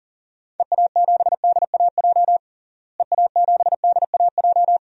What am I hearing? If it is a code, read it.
Morse code, heard as a keyed tone, about 37 wpm, 700 Hz. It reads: EA7DAJ EA7DAJ
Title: DXCC-100 Call Signs - 6 Characters - Repeated 2x - 40wpm